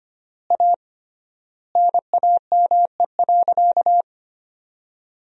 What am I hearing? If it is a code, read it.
Morse code, heard as a keyed tone, about 25 wpm, 700 Hz. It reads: A NAME.
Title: 3x Word-Spacing - Sentences from Top 500 Words - 25wpm